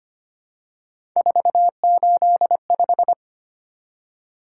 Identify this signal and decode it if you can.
Morse code, heard as a keyed tone, about 25 wpm, 700 Hz. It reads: 485
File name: Three Letter-Number Combinations - 25wpm